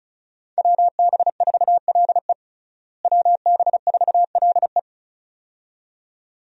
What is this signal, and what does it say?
Morse code, heard as a keyed tone, about 36 wpm, 700 Hz. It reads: WB4LE WB4LE
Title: USA Call Signs - 5 Characters - Repeated 2x - 35wpm